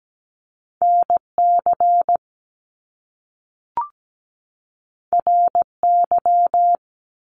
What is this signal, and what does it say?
Morse code, heard as a keyed tone, about 17 wpm, 700 Hz. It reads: NC RY
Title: Top Two Letter Combinations in Words 17wpm